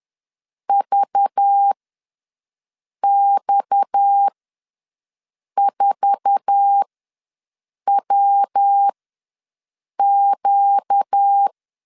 Morse code, heard as a keyed tone, about 10 wpm, spaced as for 5 wpm, 800 Hz. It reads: VX4WQ
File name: LEZIONI / 8 YJ45 / 8 YJ4 ws 10wpm 4020